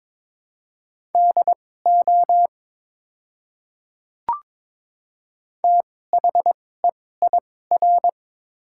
Morse code, heard as a keyed tone, about 21 wpm, spaced as for 16 wpm, 700 Hz. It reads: DO THEIR
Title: 2x Character Spacing - Top 100 Words - 22wpm